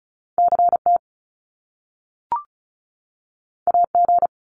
Morse code, heard as a keyed tone, about 36 wpm, 700 Hz. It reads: CT AG
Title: Top Two Letter Combinations in Words 35wpm